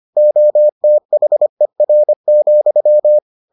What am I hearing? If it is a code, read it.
Morse code, heard as a keyed tone, about 25 wpm, 600 Hz. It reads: OTHER,